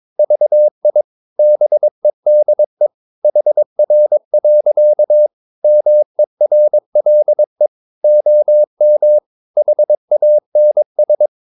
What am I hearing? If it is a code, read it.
Morse code, heard as a keyed tone, about 22 wpm, 600 Hz. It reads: VI BEDE HR. MERLE OM HANS
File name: Ferie QRS 22wpm